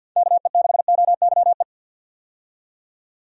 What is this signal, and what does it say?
Morse code, heard as a keyed tone, about 48 wpm, 700 Hz. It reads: KE6OYE
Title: USA Call Signs - 6 Characters - 50wpm